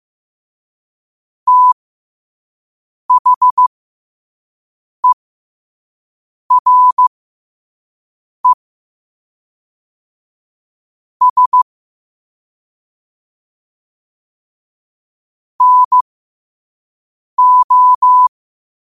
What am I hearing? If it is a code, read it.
Morse code, heard as a keyed tone, about 15 wpm, spaced as for 5 wpm, 1000 Hz. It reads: THERE S NO
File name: qotdfiles / 5wpm